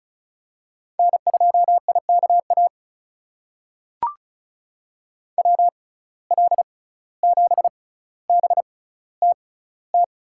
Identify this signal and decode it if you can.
Morse code, heard as a keyed tone, about 36 wpm, spaced as for 17 wpm, 700 Hz. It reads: N2IKA WL7BTT
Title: USA Call Signs - 6x Character Spacing - 35wpm